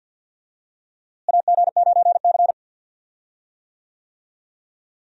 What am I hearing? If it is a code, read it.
Morse code, heard as a keyed tone, about 46 wpm, 700 Hz. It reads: AG9C
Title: USA Call Signs - 4 Characters - Repeated 2x - 50wpm